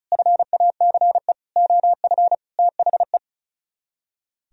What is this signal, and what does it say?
Morse code, heard as a keyed tone, about 43 wpm, 700 Hz. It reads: FACE OF THE
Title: Speed-Racing - Sets of 5 Words 35wpm